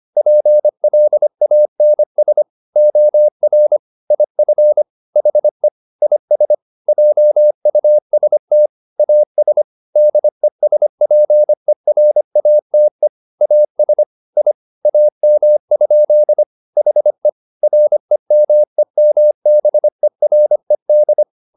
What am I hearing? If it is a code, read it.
Morse code, heard as a keyed tone, about 25 wpm, 600 Hz. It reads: PLANS OR IF HE IS JUST AS DESPERATE AS I AM? HE REMEMBERED
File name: the old man and the sea part 1 25wpm